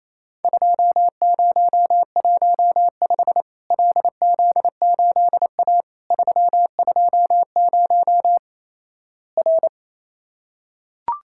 Morse code, heard as a keyed tone, about 28 wpm, 700 Hz. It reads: 2015 LZ8A 320 R
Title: Worked All Europe DX Contest - QTC Exchange - 28wpm